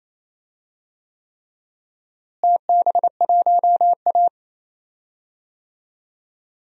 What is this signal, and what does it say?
Morse code, heard as a keyed tone, about 28 wpm, 700 Hz. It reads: TB1A
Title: DXCC-100 Call Signs - 4 Characters - Repeated 3x - 28wpm